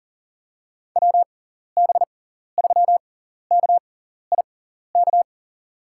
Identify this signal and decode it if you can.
Morse code, heard as a keyed tone, about 40 wpm, spaced as for 14 wpm, 700 Hz. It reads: WB3KIK
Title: USA Call Signs - 6 Characters - 6x Character Spacing - 40wpm